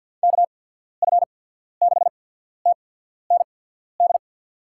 Morse code, heard as a keyed tone, about 53 wpm, spaced as for 14 wpm, 690 Hz. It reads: KF6TND
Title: USA Call Signs - 6 Characters - 8x Character Spacing - 50wpm